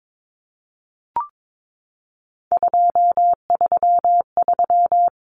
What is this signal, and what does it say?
Morse code, heard as a keyed tone, about 22 wpm, 700 Hz. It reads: 233
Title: Common Cut Numbers - 3-Digit 22wpm